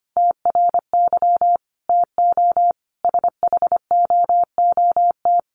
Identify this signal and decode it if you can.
Morse code, heard as a keyed tone, about 25 wpm, 700 Hz. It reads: TRY TO SHOOT
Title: Sentences from Top 1200 Words 25wpm